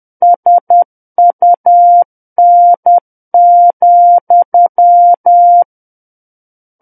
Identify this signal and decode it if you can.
Morse code, heard as a keyed tone, about 10 wpm, 700 Hz. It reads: SUN,
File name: Book One / 01-Chapter 1 - 10 WPM